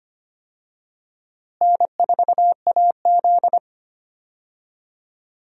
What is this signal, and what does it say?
Morse code, heard as a keyed tone, about 25 wpm, 700 Hz. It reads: N4AZ
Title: USA Call Signs - 4 Characters - Repeated 2x - 25wpm